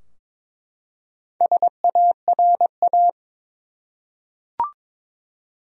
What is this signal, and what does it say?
Morse code, heard as a keyed tone, about 21 wpm, 700 Hz. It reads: SARA
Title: USA Common Names - Max 4 Letters - 22wpm